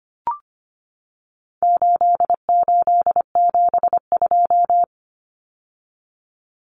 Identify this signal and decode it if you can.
Morse code, heard as a keyed tone, about 25 wpm, 700 Hz. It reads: 8872
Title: Numbers - 4-Digit 25wpm